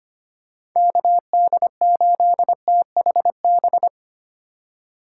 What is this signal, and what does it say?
Morse code, heard as a keyed tone, about 25 wpm, 690 Hz. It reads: KD8THB
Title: USA Call Signs - 6 Characters - 3x Character Spacing - 25wpm